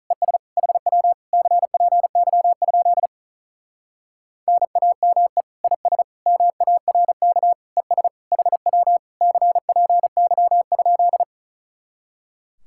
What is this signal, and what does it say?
Morse code, heard as a keyed tone, about 34 wpm, 700 Hz. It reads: ES HW CPY? NAME IS MARK ES HW CPY?
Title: Speed-Racing Common QSO Phrases - 35wpm